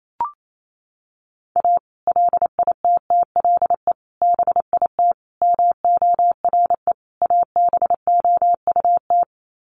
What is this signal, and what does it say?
Morse code, heard as a keyed tone, about 28 wpm, 700 Hz. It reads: A LITTLE BIT MORE ABOUT
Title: Sets of 5 Words Encore2 28wpm